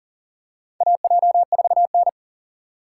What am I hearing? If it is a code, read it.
Morse code, heard as a keyed tone, about 40 wpm, 700 Hz. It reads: AJ4N
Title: USA Call Signs - 4 Characters - 4x Character Spacing - 40wpm